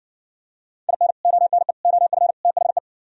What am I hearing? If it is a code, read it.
Morse code, heard as a keyed tone, about 69 wpm, 700 Hz. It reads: IN ONE OF THE